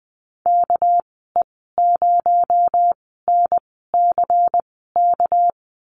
Morse code, heard as a keyed tone, about 20 wpm, spaced as for 15 wpm, 700 Hz. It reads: KE0NCK
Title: 219 Lesson 35g - Number 0 - Call Signs 20wpm